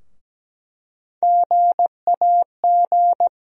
Morse code, heard as a keyed tone, about 17 wpm, 700 Hz. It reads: GAG
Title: USA Call Sign Postfixes - 3 Letters 17wpm